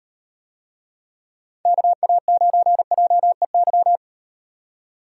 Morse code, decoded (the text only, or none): KA9JEY